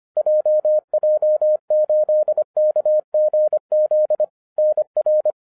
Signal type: Morse code